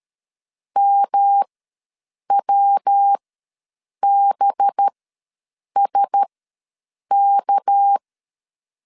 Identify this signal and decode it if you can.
Morse code, heard as a keyed tone, about 13 wpm, spaced as for 7 wpm, 800 Hz. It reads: MWBSK
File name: LEZIONI / 7 GF9Z / 7 GF9 ws 12wpm 3344